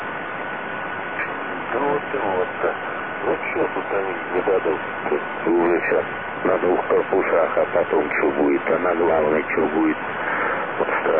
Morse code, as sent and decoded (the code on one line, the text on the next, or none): none